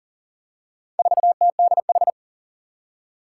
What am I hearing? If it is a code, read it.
Morse code, heard as a keyed tone, about 37 wpm, 700 Hz. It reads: 4TDH
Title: Four Challenging Letter-Number Combinations 40wpm